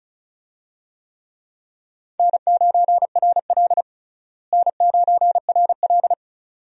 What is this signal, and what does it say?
Morse code, heard as a keyed tone, about 36 wpm, 700 Hz. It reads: N9RL N9RL
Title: USA Call Signs - 4 Characters - Repeated 2x - 35wpm